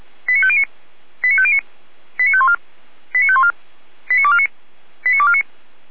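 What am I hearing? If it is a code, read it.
Five-tone selective call (ZVEI), heard as quick runs of five tones, tones of about 70 ms. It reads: 89509 89509 89514 89514 89249 89249
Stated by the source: ZVEI five-tone call received by SDR, from feuerwehr-orlamuende.org (Feuerwehr Orlamünde) archive.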